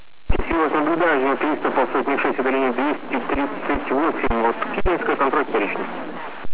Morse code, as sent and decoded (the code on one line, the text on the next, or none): none